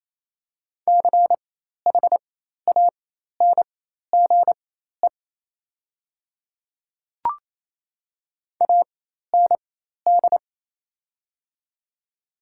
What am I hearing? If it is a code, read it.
Morse code, heard as a keyed tone, about 28 wpm, spaced as for 13 wpm, 700 Hz. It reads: CHANGE AND
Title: 4x Character Spacing - Top 200 Words - 28wpm